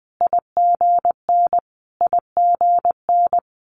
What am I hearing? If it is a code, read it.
Morse code, heard as a keyed tone, about 20 wpm, 700 Hz. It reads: IGN IGN
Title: Multi-letter Phonemes Repeated 3x 20wpm